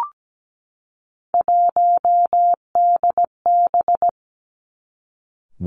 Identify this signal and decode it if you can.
Morse code, heard as a keyed tone, about 17 wpm, 700 Hz. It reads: 1DB
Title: Three Challenging Letter-Number Combinations 17wpm